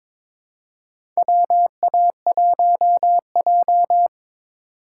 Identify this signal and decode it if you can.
Morse code, heard as a keyed tone, about 21 wpm, 700 Hz. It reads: WA1J
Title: USA Call Signs - 4x Character Spacing - 22wpm